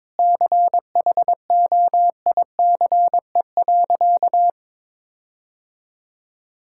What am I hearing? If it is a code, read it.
Morse code, heard as a keyed tone, about 22 wpm, 700 Hz. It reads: CHOICE.
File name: practice / 22wpm / Sentences with 8-Word Phrases - 22wpm